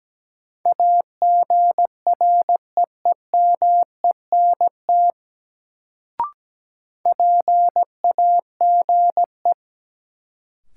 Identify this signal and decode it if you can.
Morse code, heard as a keyed tone, about 17 wpm, 700 Hz. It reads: AGREEMENT PAGE
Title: Sets of 5 Words Encore - New Word Drill - 17wpm